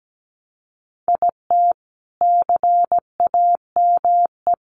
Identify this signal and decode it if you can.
Morse code, heard as a keyed tone, about 17 wpm, 700 Hz. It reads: IT CAME
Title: Sentences from Top 1000 Words 17wpm